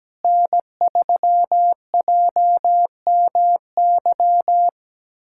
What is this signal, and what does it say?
Morse code, heard as a keyed tone, about 17 wpm, 700 Hz. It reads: N3JMY
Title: USA Callsign DE Callsign 17wpm